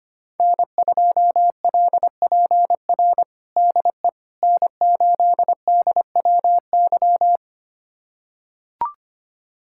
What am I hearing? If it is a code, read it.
Morse code, heard as a keyed tone, about 25 wpm, 700 Hz. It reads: N2LPR DE N8DWY
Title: USA Callsign DE Callsign - 5 Characters - 25wpm